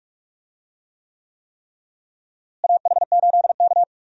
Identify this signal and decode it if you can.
Morse code, heard as a keyed tone, about 45 wpm, 700 Hz. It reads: AH8K
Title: USA Call Signs - 4 Characters - Repeated 3x - 45wpm